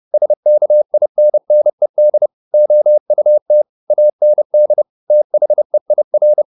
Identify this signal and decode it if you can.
Morse code, heard as a keyed tone, about 30 wpm, 600 Hz. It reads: SKINNED OUT AND THEIR
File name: the old man and the sea / the old man and the sea 1 30wpm